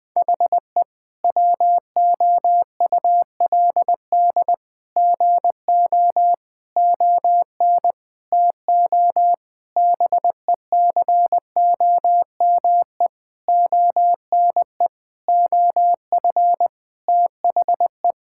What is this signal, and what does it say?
Morse code, heard as a keyed tone, about 20 wpm, 700 Hz. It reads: HE WOULD GO ON TO BECOME ONE OF THE
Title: Sentences with 8-Word Phrases - 20wpm